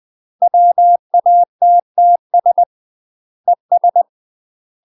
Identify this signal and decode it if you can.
Morse code, heard as a keyed tone, about 20 wpm, 700 Hz. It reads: WATTS ES
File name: UK MASTER QSOs TRAINING / 20WPM / UK MASTER QSOs 0032